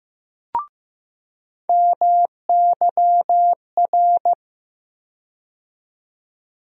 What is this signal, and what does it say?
Morse code, heard as a keyed tone, about 15 wpm, 700 Hz. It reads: MYR
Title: USA Call Sign Postfixes - 3 Letters 15wpm